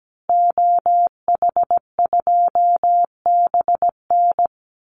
Morse code, heard as a keyed tone, about 17 wpm, 700 Hz. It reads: OH2BN
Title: Call Signs Repeated 2x - 17wpm